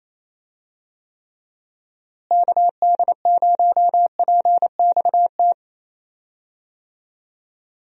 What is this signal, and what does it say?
Morse code, heard as a keyed tone, about 28 wpm, 700 Hz. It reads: KD0PXT
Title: USA Call Signs - 6 Characters - Repeated 2x - 28wpm